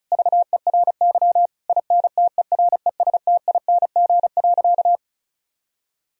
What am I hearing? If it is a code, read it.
Morse code, heard as a keyed tone, about 34 wpm, 700 Hz. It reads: VERY INTERESTING.
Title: Sentences from Top 1000 Words - Review Part 4 - 35wpm